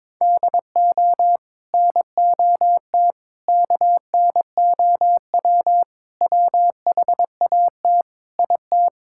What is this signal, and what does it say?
Morse code, heard as a keyed tone, about 22 wpm, 700 Hz. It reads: DO NOT KNOW WHAT IT
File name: practice / 22wpm / Build A Phrase - 5-Word Phrases - 22wpm